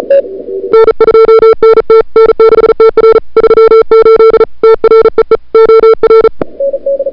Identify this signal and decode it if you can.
Morse code, heard as a keyed tone, about 36 wpm, 440 Hz. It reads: N2NT N6TR 38 TREE OR
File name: CWSprint / February 2007 CW Sprint / 01-2007 CW Sprint Hour1